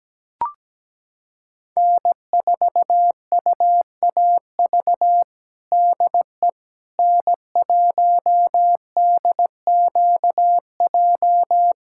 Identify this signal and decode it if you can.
Morse code, heard as a keyed tone, about 17 wpm, 700 Hz. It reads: N4UAV DE N1DQJ
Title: USA Callsign DE Callsign - 5 Characters - 17wpm